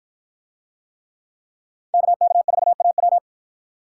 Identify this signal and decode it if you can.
Morse code, heard as a keyed tone, about 52 wpm, 700 Hz. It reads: KK4AW